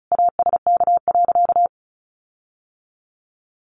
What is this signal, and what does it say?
Morse code, heard as a keyed tone, about 34 wpm, 700 Hz. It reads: ASK.
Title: Sentences from Top 1300 Words 35wpm